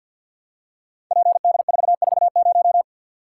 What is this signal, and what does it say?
Morse code, heard as a keyed tone, about 50 wpm, 700 Hz. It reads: PD440
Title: Code Groups - 5 Letter:Numbers - 50wpm